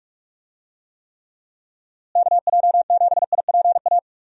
Morse code, heard as a keyed tone, about 41 wpm, 700 Hz. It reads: KJ7IPA
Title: USA Call Signs - 6 Characters - Repeated 2x - 45wpm